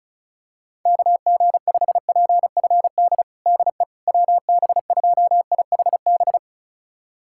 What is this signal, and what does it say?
Morse code, heard as a keyed tone, about 36 wpm, 700 Hz. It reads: KG5PFD DE WB2IHB
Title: USA Callsign DE Callsign 35wpm